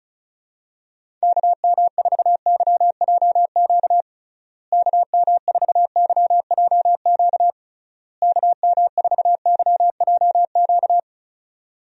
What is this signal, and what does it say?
Morse code, heard as a keyed tone, about 34 wpm, 700 Hz. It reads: KM4YJQ KM4YJQ KM4YJQ
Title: USA Call Signs - 6 Characters - Repeated 3x - 35wpm